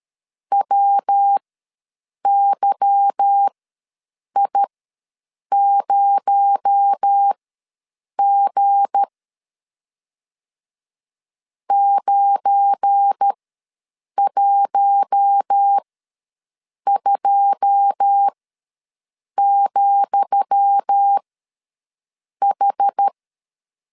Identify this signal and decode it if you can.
Morse code, heard as a keyed tone, about 13 wpm, spaced as for 7 wpm, 800 Hz. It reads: WYI0G 912,H
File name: LEZIONI / 10 SIGNS / 10D COMA ws 12wpm 3618